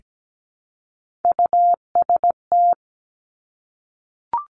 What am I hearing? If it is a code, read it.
Morse code, heard as a keyed tone, about 17 wpm, 700 Hz. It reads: UST
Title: Top Three Letter Combinations in Words 17wpm